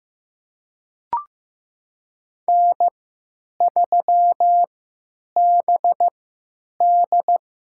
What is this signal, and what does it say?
Morse code, heard as a keyed tone, about 15 wpm, spaced as for 9 wpm, 700 Hz. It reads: N3BD
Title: USA Call Signs - 3x Character Spacing - 15wpm